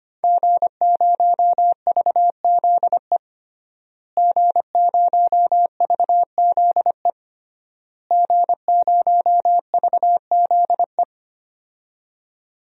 Morse code, heard as a keyed tone, about 25 wpm, 700 Hz. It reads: G0VZE G0VZE G0VZE
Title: DXCC-100 Call Signs - 5 Characters - Repeated 3x - 25wpm